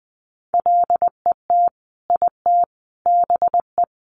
Morse code, heard as a keyed tone, about 20 wpm, 700 Hz. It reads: LET IT BE
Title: Sentences from Top 1400 Words 20wpm